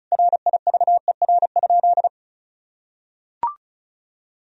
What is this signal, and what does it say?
Morse code, heard as a keyed tone, about 36 wpm, 700 Hz. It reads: RIVER?
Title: Sentences from Top 1000 Words - Review Part 1 - 35wpm